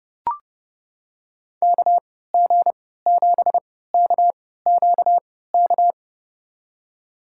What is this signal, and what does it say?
Morse code, heard as a keyed tone, about 30 wpm, spaced as for 17 wpm, 700 Hz. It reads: KG7KQK
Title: USA Call Signs - 3x Character Spacing - 30wpm